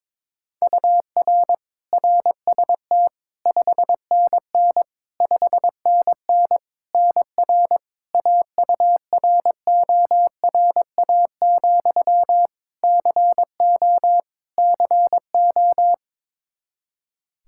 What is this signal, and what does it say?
Morse code, heard as a keyed tone, about 21 wpm, 700 Hz. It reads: UR RST 5NN 5NN NR AURORA, CO CO
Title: Practice QSO - Exchange 1 - Beginning - 22wpm